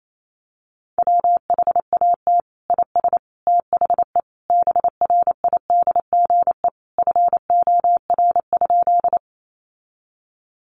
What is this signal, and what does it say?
Morse code, heard as a keyed tone, about 28 wpm, 700 Hz. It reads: WHAT IS THE BRIDGE FOR?
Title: Sentences from Top 1200 Words 28wpm